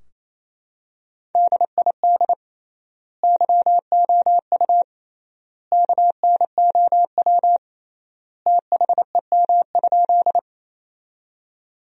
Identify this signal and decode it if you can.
Morse code, heard as a keyed tone, about 28 wpm, 700 Hz. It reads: DID YOU KNOW THEM?
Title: 3x Word-Spacing - Sentences from Top 100 Words - 28wpm